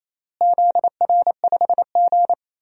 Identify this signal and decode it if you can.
Morse code, heard as a keyed tone, about 28 wpm, 700 Hz. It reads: ZR5G
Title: DXCC-100 Call Signs - 4 Characters - Repeated 3x - 28wpm